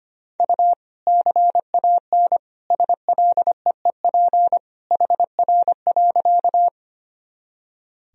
Morse code, heard as a keyed tone, about 25 wpm, 700 Hz. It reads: U CAN SLEEP HR.
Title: Sentences from Top 500 Words and Abbreviations 25wpm